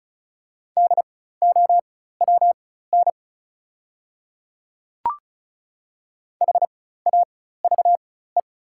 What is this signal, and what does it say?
Morse code, heard as a keyed tone, about 35 wpm, spaced as for 16 wpm, 690 Hz. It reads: DOWN HAVE
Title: 4x Character Spacing - Top 100 Words - 35wpm